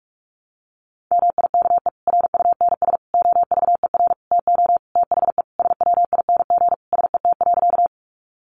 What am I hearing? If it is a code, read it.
Morse code, heard as a keyed tone, about 44 wpm, 700 Hz. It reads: MIKE RUNS OVER TO THE SWING SET.
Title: Additional Easy Phrases 45wpm